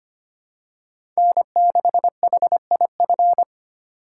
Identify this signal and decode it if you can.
Morse code, heard as a keyed tone, about 25 wpm, 700 Hz. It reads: N6HIF
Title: USA Call Signs - 3x Character Spacing - 25wpm